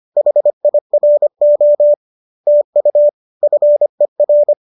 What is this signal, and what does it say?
Morse code, heard as a keyed tone, about 25 wpm, 590 Hz. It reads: HIRO TU FER
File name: qso208 25